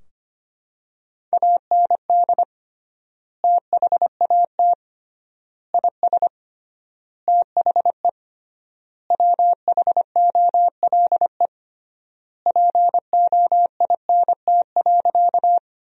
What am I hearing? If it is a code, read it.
Morse code, heard as a keyed tone, about 25 wpm, 700 Hz. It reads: AND THAT IS THE WHOLE POINT.
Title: 3x Word-Spacing - Sentences from Top 400 Words - 25wpm